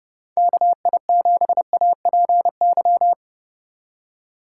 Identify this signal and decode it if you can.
Morse code, heard as a keyed tone, about 30 wpm, 700 Hz. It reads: KI7APY
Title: USA Call Signs - 6 Characters - 2x Character Spacing - 30wpm